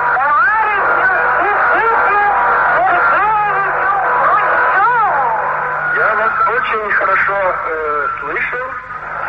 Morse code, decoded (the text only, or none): none